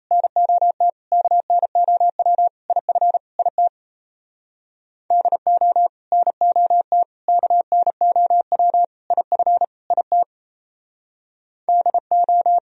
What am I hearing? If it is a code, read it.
Morse code, heard as a keyed tone, about 34 wpm, 700 Hz. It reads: NOT KNOW IF IT DO NOT KNOW IF IT DO
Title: Speed-Racing - Sets of 5 Words 28wpm